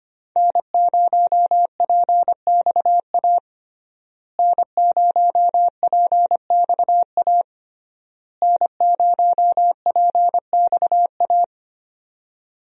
Morse code, heard as a keyed tone, about 25 wpm, 700 Hz. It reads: N0PXA N0PXA N0PXA
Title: USA Call Signs - 5 Characters - Repeated 3x - 25wpm